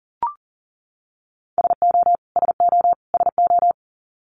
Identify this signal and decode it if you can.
Morse code, heard as a keyed tone, about 40 wpm, 700 Hz. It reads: SO SO SO
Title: Top 200 Words Repeated 40wpm